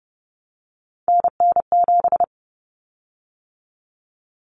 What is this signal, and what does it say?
Morse code, heard as a keyed tone, about 30 wpm, 700 Hz. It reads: NN7
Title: Common Cut Numbers - 3-Digit 30wpm